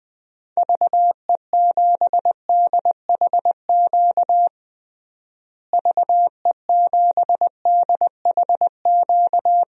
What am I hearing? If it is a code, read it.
Morse code, heard as a keyed tone, about 21 wpm, 700 Hz. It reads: VE7DHQ VE7DHQ
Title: DXCC-100 Call Signs - 6 Characters - Repeated 3x - 20wpm